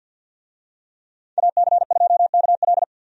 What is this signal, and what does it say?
Morse code, heard as a keyed tone, about 51 wpm, 700 Hz. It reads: AC2KL
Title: USA Call Signs - 5 Characters - 50wpm